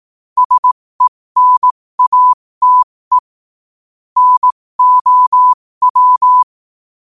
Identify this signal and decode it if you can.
Morse code, heard as a keyed tone, about 18 wpm, spaced as for 15 wpm, 1000 Hz. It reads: SENATE NOW
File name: qotdfiles / 15wpm